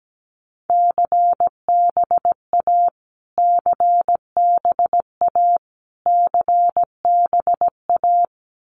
Morse code, heard as a keyed tone, about 17 wpm, 700 Hz. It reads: CBA CBA CBA
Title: Abbreviations Repeated 3x - 17wpm